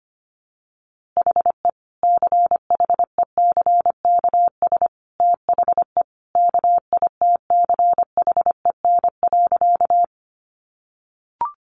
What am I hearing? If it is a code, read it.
Morse code, heard as a keyed tone, about 25 wpm, 700 Hz. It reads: HE CHECKS THE KITCHEN.
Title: Easy Phrases 25wpm